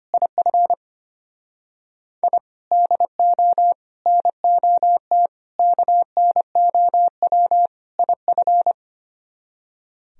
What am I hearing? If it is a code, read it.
Morse code, heard as a keyed tone, about 32 wpm, 700 Hz. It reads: IF I DO NOT KNOW IF